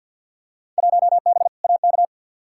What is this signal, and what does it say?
Morse code, heard as a keyed tone, about 52 wpm, 700 Hz. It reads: 1B AK
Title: Field Day Exchanges Encore - 50wpm